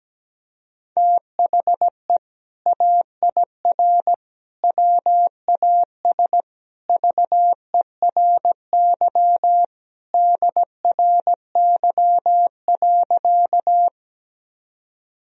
Morse code, heard as a keyed tone, about 17 wpm, 700 Hz. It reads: THE AIR WAS VERY DRY.
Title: Sentences from Top 1000 Words - Review Part 3 - 17wpm